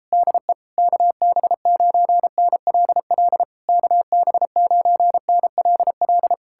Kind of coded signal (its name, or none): Morse code